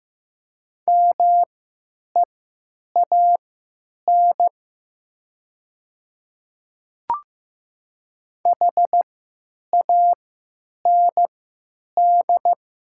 Morse code, heard as a keyed tone, about 15 wpm, spaced as for 9 wpm, 700 Hz. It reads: MEAN HAND
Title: 3x Character Spacing - Top 200 Words - 15wpm